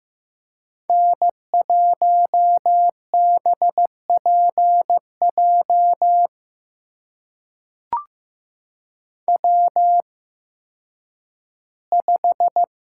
Morse code, heard as a keyed tone, about 15 wpm, spaced as for 10 wpm, 700 Hz. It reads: N1BPJ W5
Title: USA Call Signs - 5 Characters - 8x Character Spacing - 15wpm